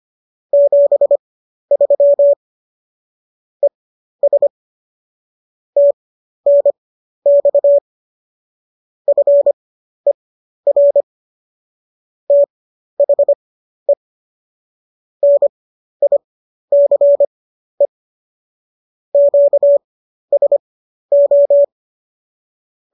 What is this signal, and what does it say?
Morse code, heard as a keyed tone, about 25 wpm, spaced as for 12 wpm, 570 Hz. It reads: 73 ES TNX FER THE NICE QSO